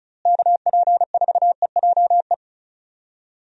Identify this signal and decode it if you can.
Morse code, heard as a keyed tone, about 36 wpm, 700 Hz. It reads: KP4EJE
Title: DXCC-100 Call Signs - 6 Characters - Repeated 3x - 35wpm